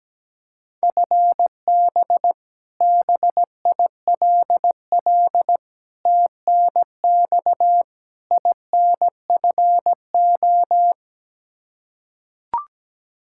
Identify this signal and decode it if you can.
Morse code, heard as a keyed tone, about 17 wpm, 700 Hz. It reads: FB BILL TNX INFO
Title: Practice QSO - Exchange 3 - More Info - 17wpm